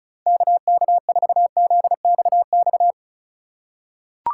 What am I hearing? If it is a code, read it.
Morse code, heard as a keyed tone, about 36 wpm, 700 Hz. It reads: KK4ZXX
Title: USA Call Signs - 6 Characters - 4x Character Spacing - 35wpm